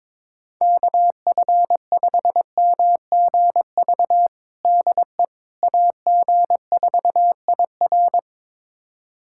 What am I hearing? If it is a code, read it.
Morse code, heard as a keyed tone, about 22 wpm, 700 Hz. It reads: KF5MGV DE AG4IR
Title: USA Callsign DE Callsign 22wpm